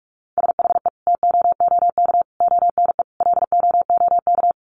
Morse code, heard as a keyed tone, about 44 wpm, 700 Hz. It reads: SHE TOOK ONE LOOK
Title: Sentences from Top 600 Words 45wpm